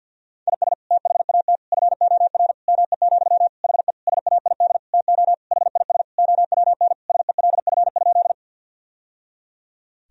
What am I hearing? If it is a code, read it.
Morse code, heard as a keyed tone, about 50 wpm, 700 Hz. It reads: IS THAT FOR ME, HE SAID TO HIS OWN SELF?
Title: Sentences from Top 1000 Words - Review Part 1 - 50wpm